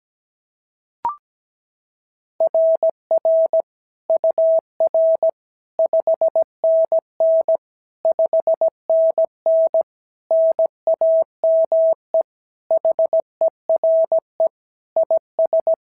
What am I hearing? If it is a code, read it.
Morse code, heard as a keyed tone, about 17 wpm, 650 Hz. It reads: RR UR 5NN 5NN NAME HERE IS
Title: SOTA QSO Practice 17wpm